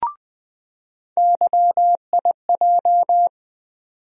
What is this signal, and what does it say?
Morse code, heard as a keyed tone, about 20 wpm, 700 Hz. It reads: YIJ
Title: USA Call Sign Postfixes - 3 Letters 20wpm